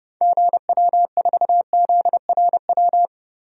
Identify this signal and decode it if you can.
Morse code, heard as a keyed tone, about 30 wpm, 700 Hz. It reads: GW4ZRW
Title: DXCC-100 Call Signs - 6 Characters - 30wpm